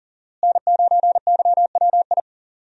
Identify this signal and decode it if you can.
Morse code, heard as a keyed tone, about 39 wpm, 700 Hz. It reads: N9YWI
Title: USA Call Signs - 5 Characters - Repeated 3x - 40wpm